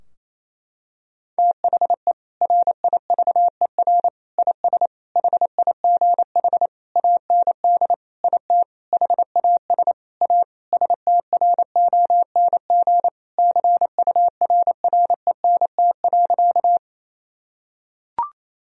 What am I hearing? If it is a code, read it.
Morse code, heard as a keyed tone, about 28 wpm, 700 Hz. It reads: THE RIVER IS HIGH AND IT HAS A STRONG CURRENT.
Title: Sentences from Top 1000 Words - Review Part 4 - 28wpm